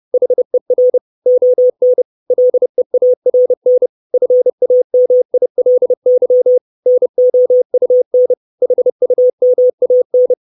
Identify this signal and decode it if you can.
Morse code, heard as a keyed tone, about 30 wpm, 500 Hz. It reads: HER ON LEARN FAMILY NOUN HUMAN
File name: 632 Words 30WPM